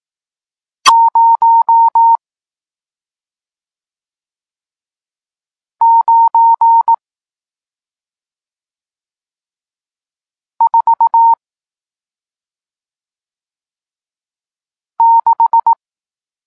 Morse code, heard as a keyed tone, about 18 wpm, 920 Hz. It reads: 0 9 4 6
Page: Tekst 40 – Nauka znaków przestankowych